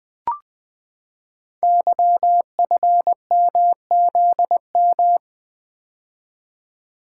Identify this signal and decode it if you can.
Morse code, heard as a keyed tone, about 20 wpm, 700 Hz. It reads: YFMZM
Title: Code Groups - 5 Letters - 20wpm